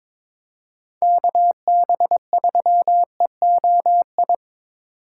Morse code, heard as a keyed tone, about 23 wpm, 700 Hz. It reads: KB3EOI
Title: USA Call Signs - 6 Characters - 4x Character Spacing - 22wpm